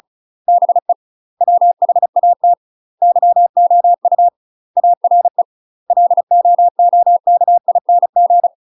Morse code, heard as a keyed tone, about 35 wpm, 700 Hz. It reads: BE WHAT YOU ARE LOOKING
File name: UK MASTER QSOs TRAINING / 35WPM / UK MASTER QSOs 0018